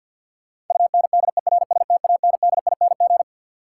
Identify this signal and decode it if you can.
Morse code, heard as a keyed tone, about 53 wpm, 700 Hz. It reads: UNDERSTANDING
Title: Sentences from Top 1000 Words - Review Part 3 - New Word Drill - 50wpm